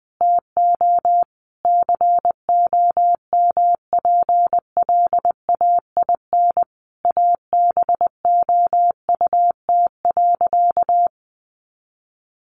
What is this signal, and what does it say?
Morse code, heard as a keyed tone, about 20 wpm, 700 Hz. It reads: TO COMPLAIN ABOUT.
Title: Sentences from Top 1300 Words 20wpm